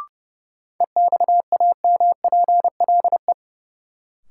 Morse code, heard as a keyed tone, about 30 wpm, 700 Hz. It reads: EXAMPLE
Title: Sets of 2 Words - New Word Drill - 30wpm